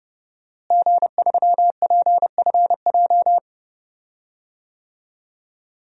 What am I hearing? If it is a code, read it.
Morse code, heard as a keyed tone, about 30 wpm, 700 Hz. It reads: G3PFJ
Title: DXCC-100 Call Signs - 5 Characters - Repeated 2x - 30wpm